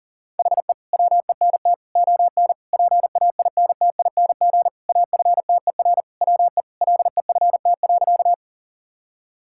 Morse code, heard as a keyed tone, about 40 wpm, 700 Hz. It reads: HE WENT ON PAINTING AFTER WE LEFT.
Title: Sentences from Top 1000 Words - Review Part 6 - 40wpm